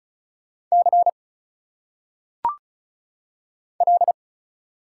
Morse code, heard as a keyed tone, about 34 wpm, 700 Hz. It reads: C L
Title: Single Letters - Phonetic Pronunciation 35wpm